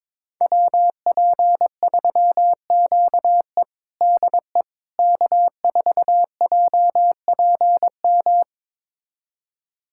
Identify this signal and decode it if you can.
Morse code, heard as a keyed tone, about 22 wpm, 700 Hz. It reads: WP3QE DE K4JPM
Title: USA Callsign DE Callsign - 5 Characters - 22wpm